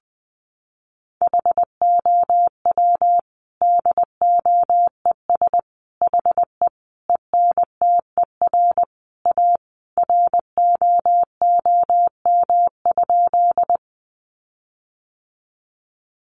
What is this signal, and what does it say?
Morse code, heard as a keyed tone, about 19 wpm, 700 Hz. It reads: HOW DOES HE ENTER A ROOM?
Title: Sentences from Top 1300 Words 20wpm